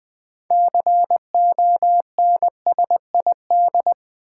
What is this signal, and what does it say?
Morse code, heard as a keyed tone, about 20 wpm, 700 Hz. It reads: CONSID
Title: Build A Word - 6 Letters At A Time - 12-Letter Words - 20wpm